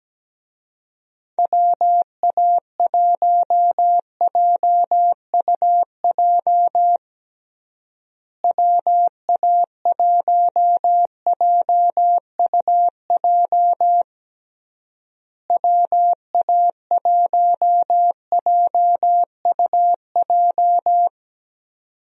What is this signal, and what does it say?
Morse code, heard as a keyed tone, about 17 wpm, 700 Hz. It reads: WA1JUJ WA1JUJ WA1JUJ
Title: USA Call Signs - 6 Characters - Repeated 3x - 17wpm